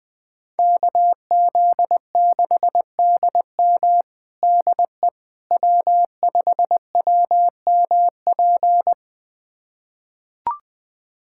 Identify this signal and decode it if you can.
Morse code, heard as a keyed tone, about 21 wpm, 700 Hz. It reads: KZ6DM DE W5WMP
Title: USA Callsign DE Callsign - 5 Characters - 20wpm